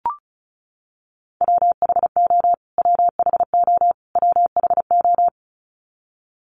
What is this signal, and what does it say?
Morse code, heard as a keyed tone, about 34 wpm, 700 Hz. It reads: WHO WHO WHO
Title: Top 200 Words Repeated 35wpm